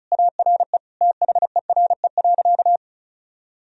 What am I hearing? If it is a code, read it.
Morse code, heard as a keyed tone, about 36 wpm, 700 Hz. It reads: ARE THERE.
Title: Sentences from Top 1000 Words - Review Part 1 - 35wpm